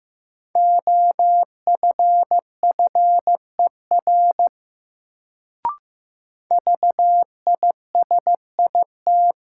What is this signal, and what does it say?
Morse code, heard as a keyed tone, about 15 wpm, 700 Hz. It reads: OFFER VISIT
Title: Sentences from Top 900 Words - New Word Drill - 15wpm